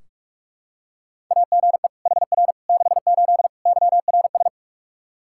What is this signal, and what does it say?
Morse code, heard as a keyed tone, about 46 wpm, 700 Hz. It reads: AGE HR 68 YRS
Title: K1USN SST QSO Practice 45wpm